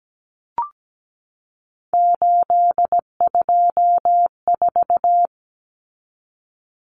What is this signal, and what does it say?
Morse code, heard as a keyed tone, about 17 wpm, 700 Hz. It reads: 824
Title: Numbers - 3-Digit 17wpm